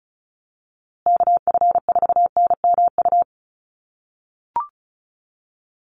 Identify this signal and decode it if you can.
Morse code, heard as a keyed tone, about 34 wpm, 700 Hz. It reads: KF4NMU
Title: USA Call Signs Repeated 3x 35wpm